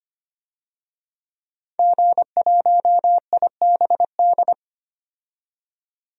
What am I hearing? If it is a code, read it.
Morse code, heard as a keyed tone, about 25 wpm, 700 Hz. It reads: G1IBD
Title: DXCC-100 Call Signs - 5 Characters - 25wpm